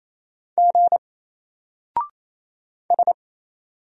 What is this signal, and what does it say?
Morse code, heard as a keyed tone, about 27 wpm, 700 Hz. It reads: G S
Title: Single Character (Letter:Number:Punctuation:Prosign) 28wpm